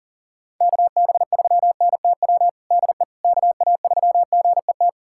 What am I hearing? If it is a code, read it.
Morse code, heard as a keyed tone, about 40 wpm, 700 Hz. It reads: KB3NTW DE KA3GET
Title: USA Callsign DE Callsign - 6 Characters - 40wpm